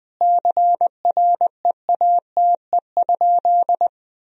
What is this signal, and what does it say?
Morse code, heard as a keyed tone, about 20 wpm, 700 Hz. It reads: CREATE?
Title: Sentences from Top 1000 Words - Review Part 3 - 20wpm